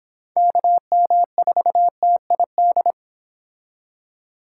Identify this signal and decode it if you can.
Morse code, heard as a keyed tone, about 26 wpm, 700 Hz. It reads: KM4TID